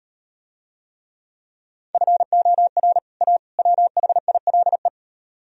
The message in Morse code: FOR A WHILE

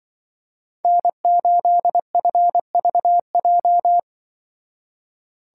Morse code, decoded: N8FVJ

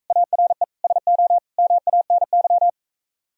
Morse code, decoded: ARE SO MANY

